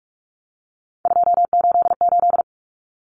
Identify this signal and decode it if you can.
Morse code, heard as a keyed tone, about 44 wpm, 700 Hz. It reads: 288
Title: Common Cut Numbers - 3-Digit 45wpm